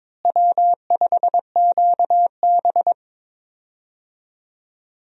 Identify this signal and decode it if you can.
Morse code, heard as a keyed tone, about 22 wpm, 700 Hz. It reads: W5QB
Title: USA Call Signs - 4 Characters - Repeated 3x - 22wpm